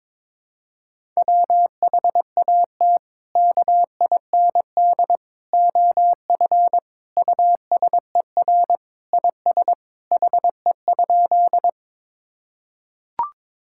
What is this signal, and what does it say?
Morse code, heard as a keyed tone, about 22 wpm, 700 Hz. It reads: WHAT KIND OF USER IS HE?
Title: Sentences from Top 1000 Words - Review Part 5 - 22wpm